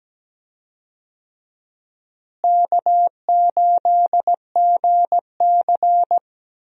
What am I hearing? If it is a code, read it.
Morse code, heard as a keyed tone, about 17 wpm, 700 Hz. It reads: K8GC
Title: USA Call Signs - 4 Characters - Repeated 3x - 17wpm